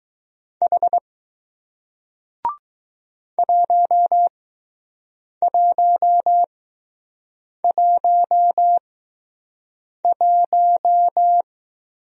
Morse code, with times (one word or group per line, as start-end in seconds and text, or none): 0.6-1.0 H
3.4-11.4 1111